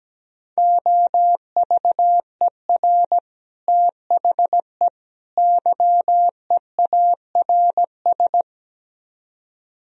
Morse code, heard as a keyed tone, about 17 wpm, 700 Hz. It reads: OVER THE YEARS
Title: Speed-Racing - Sets of 3 Words 17wpm